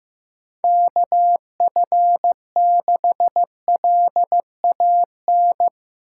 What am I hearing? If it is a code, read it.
Morse code, heard as a keyed tone, about 15 wpm, 700 Hz. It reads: KF6LAN
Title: USA Call Signs - 6 Characters - 6x Character Spacing - 15wpm